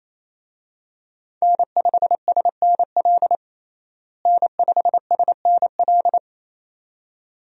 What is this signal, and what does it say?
Morse code, heard as a keyed tone, about 28 wpm, 700 Hz. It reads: N5SNL N5SNL
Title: USA Call Signs - 5 Characters - Repeated 2x - 28wpm